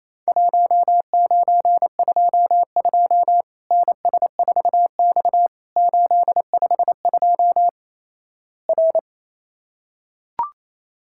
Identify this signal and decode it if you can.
Morse code, heard as a keyed tone, about 28 wpm, 700 Hz. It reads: 1922 NS4X 852 R
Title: Worked All Europe DX Contest - QTC Exchange - 28wpm